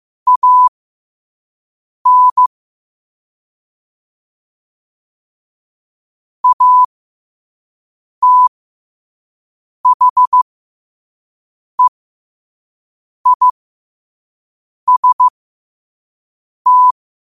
Morse code, heard as a keyed tone, about 15 wpm, spaced as for 5 wpm, 1000 Hz. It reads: AN ATHEIST